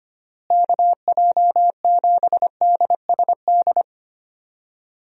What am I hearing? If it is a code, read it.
Morse code, heard as a keyed tone, about 25 wpm, 700 Hz. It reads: KJ7DSD
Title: USA Call Signs - 6 Characters - Repeated 3x - 25wpm